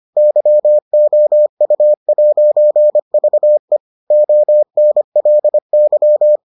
Morse code, heard as a keyed tone, about 25 wpm, 600 Hz. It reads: YOU'VE ONLY